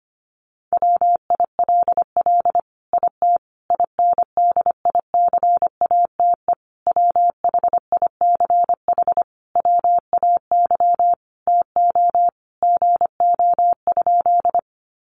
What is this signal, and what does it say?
Morse code, heard as a keyed tone, about 25 wpm, 700 Hz. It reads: WILL IT INDICATE WHICH WAY TO GO?
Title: Sentences from Top 1300 Words 25wpm